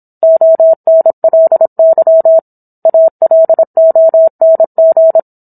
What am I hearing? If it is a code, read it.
Morse code, heard as a keyed tone, about 27 wpm, 650 Hz. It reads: ONLY ALONG